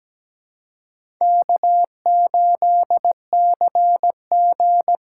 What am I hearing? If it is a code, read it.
Morse code, heard as a keyed tone, about 17 wpm, 700 Hz. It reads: K8CG
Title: USA Callsign DE Callsign - 4 Characters - 17wpm